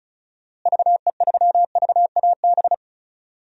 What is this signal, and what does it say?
Morse code, heard as a keyed tone, about 36 wpm, 700 Hz. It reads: VE3VAB
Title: DXCC-100 Call Signs - 6 Characters - Repeated 3x - 35wpm